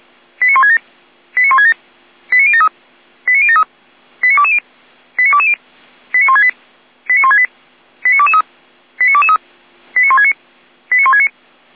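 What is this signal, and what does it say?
Five-tone selective call (ZVEI), heard as quick runs of five tones, tones of about 70 ms. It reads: 89167 89167 89073 89073 89220 89220 89168 89168 89223 89223 89169 89169
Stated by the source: ZVEI five-tone selective call received by SDR.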